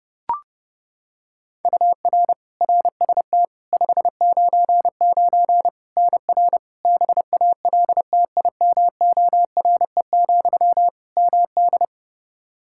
Morse code, heard as a keyed tone, about 30 wpm, 700 Hz. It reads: UR RST 599 NR BALTIMORE, MD
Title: Practice QSO - Exchange 1 - Beginning - 30wpm